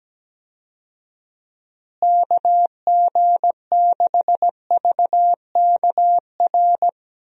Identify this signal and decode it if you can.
Morse code, heard as a keyed tone, about 17 wpm, 700 Hz. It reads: KG6VKR
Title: USA Call Signs - 6 Characters - Repeated 3x - 17wpm